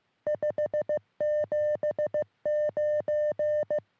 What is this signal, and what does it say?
Morse code, heard as a keyed tone, about 15 wpm, 600 Hz. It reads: 579
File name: sm7lcb / oe5vrl-5 23cm